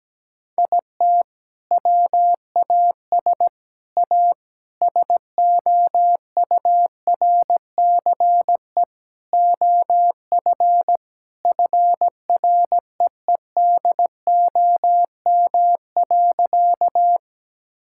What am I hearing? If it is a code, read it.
Morse code, heard as a keyed tone, about 17 wpm, 700 Hz. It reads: IT WAS A SOURCE OF FREEDOM.
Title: Sentences from Top 1000 Words - Review Part 5 - 17wpm